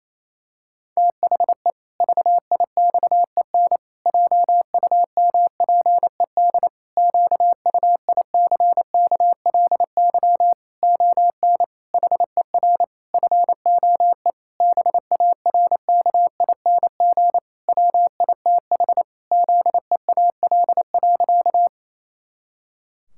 Morse code, heard as a keyed tone, about 28 wpm, 700 Hz. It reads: THE VIXEN JUMPED QUICKLY ON HER FOE BARKING WITH ZEAL.
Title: Pangram Challenge v2 - 28wpm